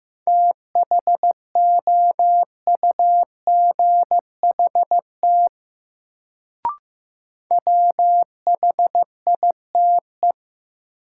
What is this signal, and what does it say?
Morse code, heard as a keyed tone, about 15 wpm, 700 Hz. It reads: THOUGHT WHITE
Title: Sentences from Top 300 Words - New Word Drill - 15wpm